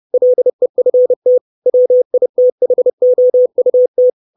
Morse code, heard as a keyed tone, about 30 wpm, 500 Hz. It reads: LEFT WITHOUT
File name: 632 Words 30WPM